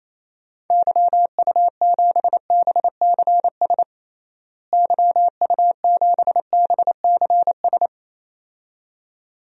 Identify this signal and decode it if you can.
Morse code, heard as a keyed tone, about 28 wpm, 700 Hz. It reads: YU7BCS YU7BCS
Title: DXCC-100 Call Signs - 6 Characters - Repeated 2x - 28wpm